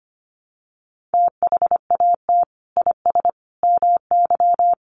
Morse code, heard as a keyed tone, about 25 wpm, 700 Hz. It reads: THAT IS MY
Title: Sentences from Top 700 Words 25wpm